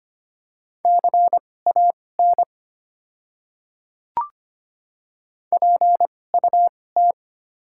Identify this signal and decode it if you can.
Morse code, heard as a keyed tone, about 25 wpm, spaced as for 18 wpm, 690 Hz. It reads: CAN PUT
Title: 2x Character Spacing - Top 200 Words - 25wpm